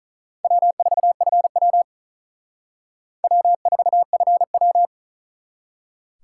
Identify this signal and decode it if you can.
Morse code, heard as a keyed tone, about 42 wpm, 700 Hz. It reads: W4FW W4FW
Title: Speed Racing - USA 1x2 Call Signs 35wpm